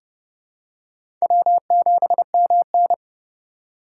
Morse code, heard as a keyed tone, about 30 wpm, 700 Hz. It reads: W7MN
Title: USA Call Signs - 4 Characters - 4x Character Spacing - 30wpm